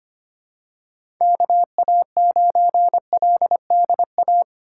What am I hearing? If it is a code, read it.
Morse code, heard as a keyed tone, about 25 wpm, 700 Hz. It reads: KA9LDA
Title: USA Call Signs - 6 Characters - 2x Character Spacing - 25wpm